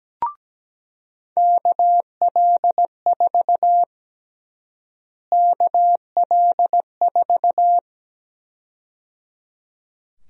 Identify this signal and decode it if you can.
Morse code, heard as a keyed tone, about 17 wpm, 700 Hz. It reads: KL4 KL4
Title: USA Call Sign Prefixes - Repeated 2x - 17wpm